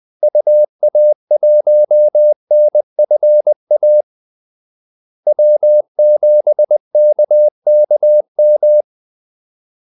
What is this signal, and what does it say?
Morse code, heard as a keyed tone, about 20 wpm, 600 Hz. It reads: UA1NFA W7KKM
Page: Kaldesignaler 20 wpm